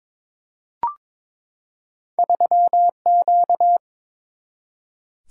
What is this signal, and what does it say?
Morse code, heard as a keyed tone, about 22 wpm, 700 Hz. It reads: 3Q
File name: practice / 22wpm / Two Letter-Number Combinations - 22wpm